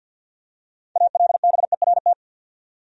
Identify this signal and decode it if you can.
Morse code, heard as a keyed tone, about 52 wpm, 700 Hz. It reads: ALBERT
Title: USA Common Names - 50wpm